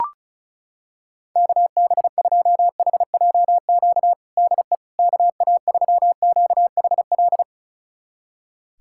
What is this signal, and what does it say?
Morse code, heard as a keyed tone, about 34 wpm, 700 Hz. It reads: KB2HJQ DE KA3QHL
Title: USA Callsign DE Callsign 35wpm